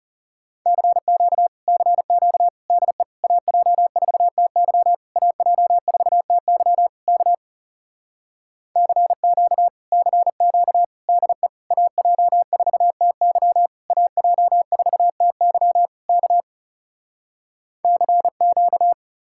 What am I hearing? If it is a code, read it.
Morse code, heard as a keyed tone, about 37 wpm, 700 Hz. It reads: CQ CQ DE AJ4TY AJ4TY K CQ CQ DE AJ4TY AJ4TY K CQ
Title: Speed-Racing Common QSO Phrases Encore - 30wpm